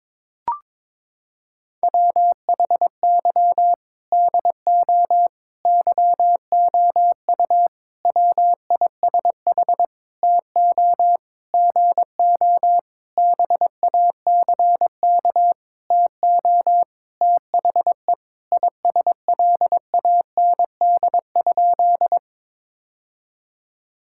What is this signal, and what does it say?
Morse code, heard as a keyed tone, about 22 wpm, 700 Hz. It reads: WHY DO YOU WISH TO GO BACK TO THE ISLAND?
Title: Sentences from Top 1600 Words 22wpm